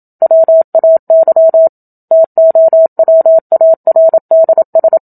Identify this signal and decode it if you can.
Morse code, heard as a keyed tone, about 27 wpm, 650 Hz. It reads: WAY TOWARDS
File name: Book Two / 01-Chapter 1 - 27 WPM